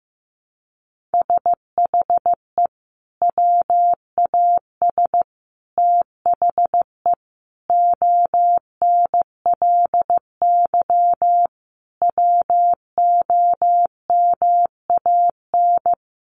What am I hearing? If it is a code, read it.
Morse code, heard as a keyed tone, about 14 wpm, 700 Hz. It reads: SHE WAS THE ONLY WOMAN
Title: Sentences from Top 1100 Words 15wpm